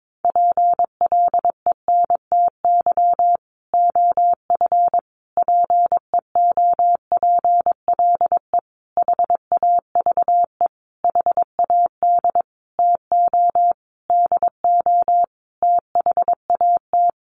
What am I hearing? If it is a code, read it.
Morse code, heard as a keyed tone, about 22 wpm, 700 Hz. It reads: PLENTY OF PEOPLE HAVE HAD TO DO THAT
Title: Sentences from Top 1300 Words 22wpm